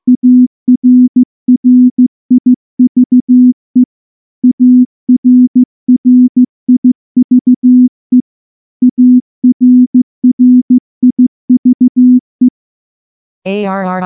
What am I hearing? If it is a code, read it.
Morse code, heard as a keyed tone, about 15 wpm, 250 Hz. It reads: ARRIVE ARRIVE ARRIVE
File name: morse / morse-training-15WPM